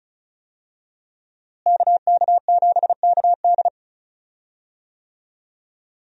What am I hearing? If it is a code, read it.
Morse code, heard as a keyed tone, about 36 wpm, 700 Hz. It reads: KK7KD
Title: USA Call Signs - 5 Characters - Repeated 3x - 35wpm